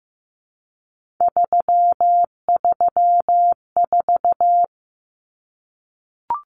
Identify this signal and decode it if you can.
Morse code, heard as a keyed tone, about 15 wpm, 700 Hz. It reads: 334
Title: Numbers - 3-Digit 15wpm